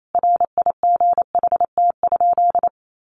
Morse code, heard as a keyed tone, about 28 wpm, 700 Hz. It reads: RIGHT?
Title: Sentences from Top 1000 Words 28wpm